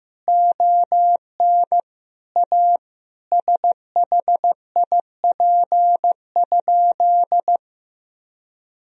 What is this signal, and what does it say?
Morse code, heard as a keyed tone, about 15 wpm, 700 Hz. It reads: ON A SHIP?
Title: Sentences from Top 1000 Words - Review Part 2 - 15wpm